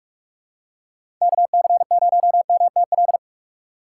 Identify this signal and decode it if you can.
Morse code, heard as a keyed tone, about 46 wpm, 700 Hz. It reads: KC0MTL
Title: USA Call Signs - 6 Characters - 6x Character Spacing - 45wpm